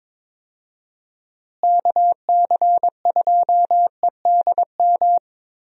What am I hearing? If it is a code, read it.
Morse code, heard as a keyed tone, about 22 wpm, 700 Hz. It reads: KC2EDM
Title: USA Call Signs - 6 Characters - Repeated 2x - 22wpm